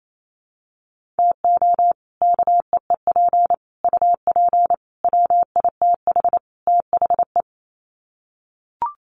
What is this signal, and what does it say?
Morse code, heard as a keyed tone, about 28 wpm, 700 Hz. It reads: TO KEEP UP WITH THE
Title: Sets of 5 Words Encore 28wpm